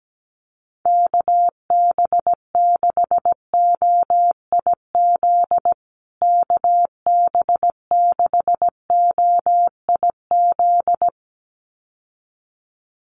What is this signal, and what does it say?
Morse code, heard as a keyed tone, about 17 wpm, 700 Hz. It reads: KB6OIZ KB6OIZ
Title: USA Call Signs Repeated 2x 17wpm